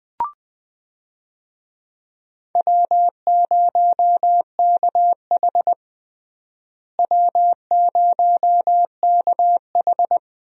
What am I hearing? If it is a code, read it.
Morse code, heard as a keyed tone, about 21 wpm, 700 Hz. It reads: W0KH W0KH
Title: USA Call Signs - 4 Characters - Repeated 2x - 20wpm